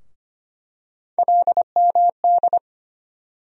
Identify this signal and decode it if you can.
Morse code, heard as a keyed tone, about 25 wpm, 700 Hz. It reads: LMD
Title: USA Call Sign Postfixes - 3 Letters 25wpm